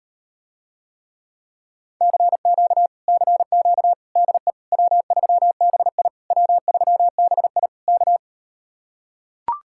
Morse code, heard as a keyed tone, about 38 wpm, 700 Hz. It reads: CQ CQ DE W3BI W3BI K